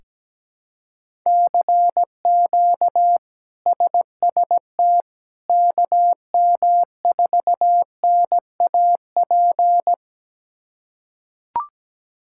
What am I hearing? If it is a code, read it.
Morse code, heard as a keyed tone, about 17 wpm, 700 Hz. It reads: CQ SST KM4NAP
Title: K1USN SST QSO Practice - 17wpm